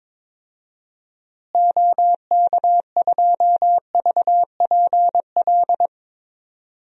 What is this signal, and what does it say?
Morse code, heard as a keyed tone, about 22 wpm, 700 Hz. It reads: OK2VPL
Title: DXCC-100 Call Signs - 6 Characters - Repeated 2x - 22wpm